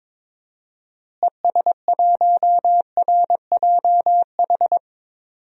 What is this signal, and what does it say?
Morse code, heard as a keyed tone, about 21 wpm, 700 Hz. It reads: ES1RJH
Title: DXCC-100 Call Signs - 6 Characters - Repeated 2x - 22wpm